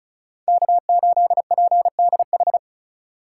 Morse code, heard as a keyed tone, about 34 wpm, 700 Hz. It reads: K8PDH
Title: USA Call Signs - 5 Characters - 2x Character Spacing - 35wpm